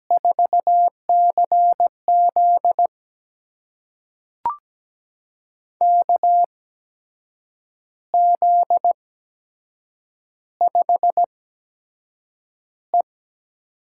Morse code, heard as a keyed tone, about 17 wpm, spaced as for 7 wpm, 700 Hz. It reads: 4CZ KZ5E
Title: USA Call Signs - 8x Character Spacing - 17wpm